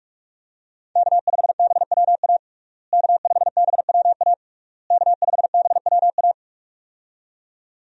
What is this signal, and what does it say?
Morse code, heard as a keyed tone, about 42 wpm, 700 Hz. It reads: K5BWA K5BWA K5BWA
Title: USA Call Signs - 5 Characters - Repeated 3x - 45wpm